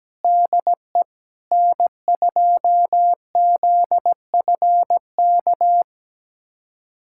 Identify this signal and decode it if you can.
Morse code, heard as a keyed tone, about 17 wpm, 700 Hz. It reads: DE N2ZFK
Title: USA Callsign DE Callsign - 5 Characters - 17wpm